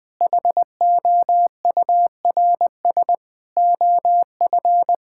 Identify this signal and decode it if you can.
Morse code, heard as a keyed tone, about 20 wpm, 700 Hz. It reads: HOURS OF
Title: Sets of 6 Words - 20wpm